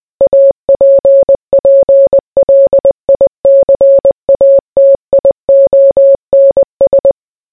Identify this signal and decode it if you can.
Morse code, heard as a keyed tone, about 20 wpm, 550 Hz. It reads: APPLICATIONS